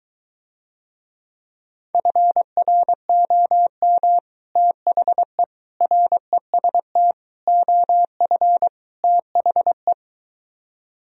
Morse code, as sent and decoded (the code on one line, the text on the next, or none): ..-. .-. --- -- / - .... / .-. ... - / --- ..-. / - .... .
FROM THE REST OF THE